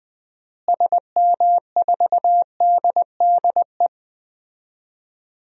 Morse code, heard as a keyed tone, about 20 wpm, 700 Hz. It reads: SM4DDE
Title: DXCC-100 Call Signs - 6 Characters - Repeated 2x - 20wpm